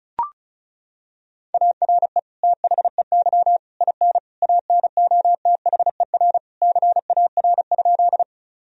Morse code, heard as a keyed tone, about 34 wpm, 700 Hz. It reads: ARE THEY IN ANOTHER CAR?
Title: Sentences from Top 1000 Words - Review Part 1 - 35wpm